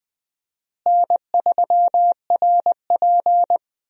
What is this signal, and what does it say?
Morse code, heard as a keyed tone, about 20 wpm, 700 Hz. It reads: N3RP
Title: USA Call Signs - 4 Characters - Repeated 2x - 20wpm